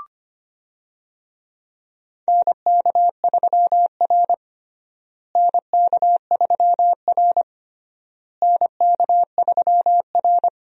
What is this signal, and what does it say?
Morse code, heard as a keyed tone, about 25 wpm, 700 Hz. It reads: NK3R NK3R NK3R
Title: USA Call Signs - 4 Characters - Repeated 3x - 25wpm